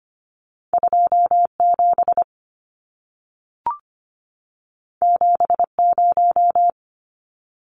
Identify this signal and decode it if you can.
Morse code, heard as a keyed tone, about 25 wpm, 700 Hz. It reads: 27 70
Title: Numbers - 2-Digit 25wpm